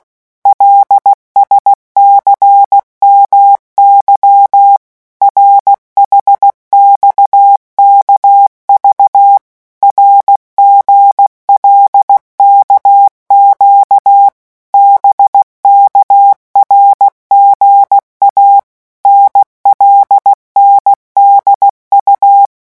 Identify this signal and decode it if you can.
Morse code, heard as a keyed tone, about 16 wpm, 800 Hz. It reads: LSCMY RHXKV RGLKQ BKRGA NLNDU
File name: morse mp3 / revision5-vitesse 15 mots